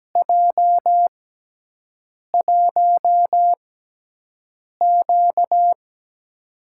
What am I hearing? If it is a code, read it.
Morse code, heard as a keyed tone, about 17 wpm, spaced as for 6 wpm, 700 Hz. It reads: J1Q
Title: USA Call Signs - 6x Character Spacing - 17wpm